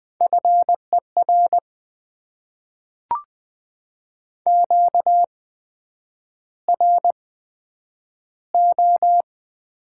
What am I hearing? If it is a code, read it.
Morse code, heard as a keyed tone, about 20 wpm, spaced as for 9 wpm, 700 Hz. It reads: FER QRO
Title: Top 90 QSO Elements - 8x Character Spacing - 20wpm